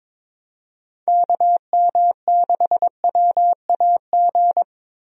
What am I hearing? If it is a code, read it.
Morse code, heard as a keyed tone, about 21 wpm, 700 Hz. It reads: KM6WAG
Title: USA Call Signs - 6 Characters - Repeated 3x - 22wpm